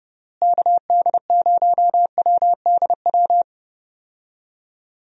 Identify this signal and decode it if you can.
Morse code, heard as a keyed tone, about 32 wpm, 700 Hz. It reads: KD0WDW
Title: USA Call Signs - 6 Characters - Repeated 3x - 30wpm